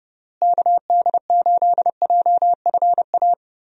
Morse code, heard as a keyed tone, about 32 wpm, 700 Hz. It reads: KD8JFA
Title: USA Call Signs - 6 Characters - Repeated 3x - 30wpm